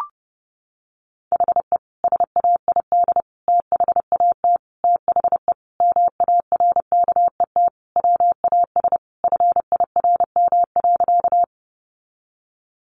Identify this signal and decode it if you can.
Morse code, heard as a keyed tone, about 28 wpm, 700 Hz. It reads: HE SAID THAT THE MARKET WAS FIRM.
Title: Sentences from Top 1400 Words 30wpm